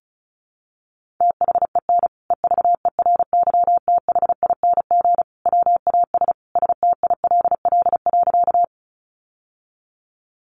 Morse code, heard as a keyed tone, about 34 wpm, 700 Hz. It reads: THEN EVERYTHING WAS STILL.
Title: Sentences from Top 700 Words 35wpm